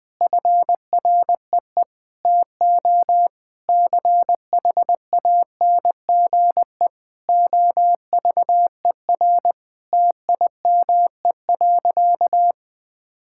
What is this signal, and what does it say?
Morse code, heard as a keyed tone, about 21 wpm, 700 Hz. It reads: FREE TO CHANGE OVER TIME.
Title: Sentences from Top 1000 Words - Review Part 6 - 20wpm